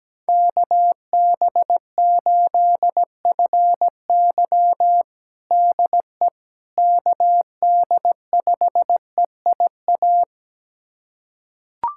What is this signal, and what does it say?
Morse code, heard as a keyed tone, about 17 wpm, 700 Hz. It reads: KB8FY DE KD5EIA
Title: USA Callsign DE Callsign 17wpm